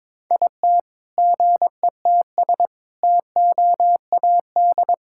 Morse code, heard as a keyed tone, about 22 wpm, 700 Hz. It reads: IT GETS TOAD
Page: morse-code-ninja-practice-22wpm